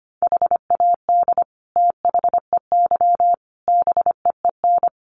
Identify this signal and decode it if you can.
Morse code, heard as a keyed tone, about 25 wpm, 700 Hz. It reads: HAD THEY BEEN
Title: Sentences from Top 400 Words 25wpm